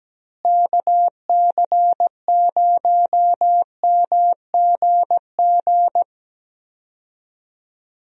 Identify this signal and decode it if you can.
Morse code, heard as a keyed tone, about 17 wpm, 700 Hz. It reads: KC0MGG
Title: USA Call Signs - 6 Characters - Repeated 2x - 17wpm